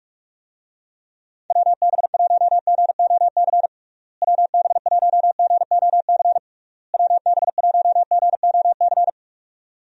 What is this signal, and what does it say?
Morse code, heard as a keyed tone, about 46 wpm, 700 Hz. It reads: WB1GOC WB1GOC WB1GOC
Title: USA Call Signs - 6 Characters - Repeated 3x - 45wpm